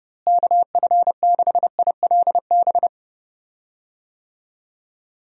Morse code, heard as a keyed tone, about 30 wpm, 700 Hz. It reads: KF6ILB
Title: USA Call Signs - 6 Characters - Repeated 2x - 30wpm